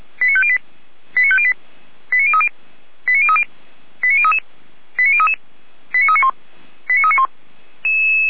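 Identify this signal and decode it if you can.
Five-tone selective call (ZVEI), heard as quick runs of five tones, tones of about 70 ms. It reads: 89508 89508 89030 89030 89033 89033 89391 89391
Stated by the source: ZVEI five-tone selective call to address 89050, received on an SDR.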